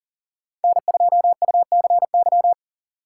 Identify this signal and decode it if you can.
Morse code, heard as a keyed tone, about 40 wpm, 700 Hz. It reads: N2UCY
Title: USA Call Signs - 5 Characters - 4x Character Spacing - 40wpm